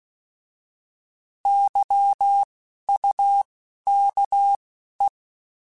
Morse code, heard as a keyed tone, about 16 wpm, 800 Hz. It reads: Y U K E
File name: Alphabetical / Alphabetical mp3s / Ys @10